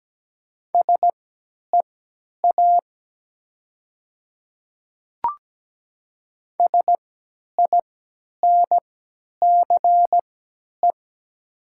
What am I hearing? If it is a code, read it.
Morse code, heard as a keyed tone, about 17 wpm, spaced as for 10 wpm, 700 Hz. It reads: SEA SINCE
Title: 3x Character Spacing - Top 300 Words - 17wpm